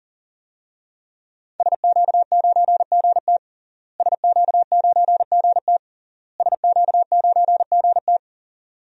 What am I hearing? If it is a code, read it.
Morse code, heard as a keyed tone, about 40 wpm, 700 Hz. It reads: SQ9GT SQ9GT SQ9GT
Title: DXCC-100 Call Signs - 5 Characters - Repeated 3x - 40wpm